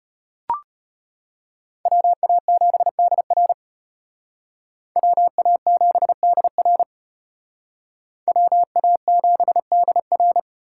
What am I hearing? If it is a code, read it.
Morse code, heard as a keyed tone, about 32 wpm, 700 Hz. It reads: WA7DR WA7DR WA7DR